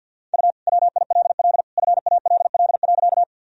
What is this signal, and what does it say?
Morse code, heard as a keyed tone, about 52 wpm, 700 Hz. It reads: U WILL FALL.
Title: Sentences from Top 500 Words and Abbreviations 50wpm